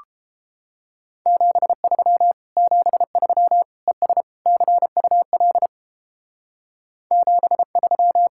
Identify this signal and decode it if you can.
Morse code, heard as a keyed tone, about 29 wpm, 700 Hz. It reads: 73 73 ES CUL 73